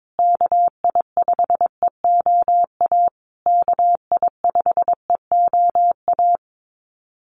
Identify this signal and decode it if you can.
Morse code, heard as a keyed tone, about 22 wpm, 700 Hz. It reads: KI5EOA KI5EOA
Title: USA Call Signs Repeated 3x 22wpm